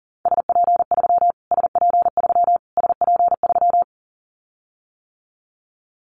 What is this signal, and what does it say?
Morse code, heard as a keyed tone, about 40 wpm, 700 Hz. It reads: SP3 SP3 SP3
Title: Callsign Prefixes for Top 25 DXCC - Repeated 3x - 40wpm